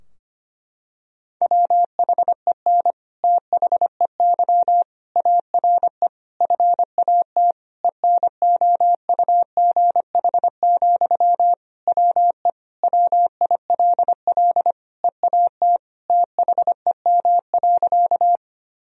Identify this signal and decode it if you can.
Morse code, heard as a keyed tone, about 25 wpm, 700 Hz. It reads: WHEN THEY ARE FAT ENOUGH, WE WILL EAT THEM.
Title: Sentences from Top 1000 Words - Review Part 4 - 25wpm